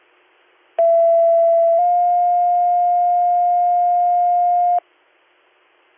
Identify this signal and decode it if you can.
Two-tone page, heard as a one-second tone, then a three-second tone, about 670 then 710 Hz. It is heard from 0.8 to 4.8 s.